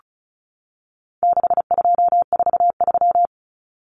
Morse code, heard as a keyed tone, about 34 wpm, 700 Hz. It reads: 6243
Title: Numbers - 4-Digit 35wpm